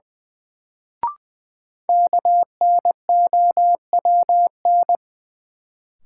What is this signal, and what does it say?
Morse code, heard as a keyed tone, about 20 wpm, 700 Hz. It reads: KNOWN
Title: Sentences from Top 600 Words - New Word Drill - 20wpm